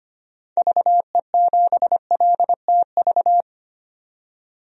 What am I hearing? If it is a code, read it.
Morse code, heard as a keyed tone, about 25 wpm, 700 Hz. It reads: VE7LTV
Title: DXCC-100 Call Signs - 6 Characters - 25wpm